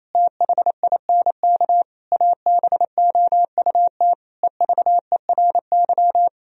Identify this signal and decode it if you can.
Morse code, heard as a keyed tone, about 28 wpm, 700 Hz. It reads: THINK ABOUT EVERY
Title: Sentences from Top 1000 Words - Review Part 1 - 28wpm